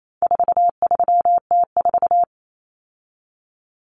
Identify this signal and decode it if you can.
Morse code, heard as a keyed tone, about 28 wpm, 700 Hz. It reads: 43T4
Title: Common Cut Numbers - 4-Digit 28wpm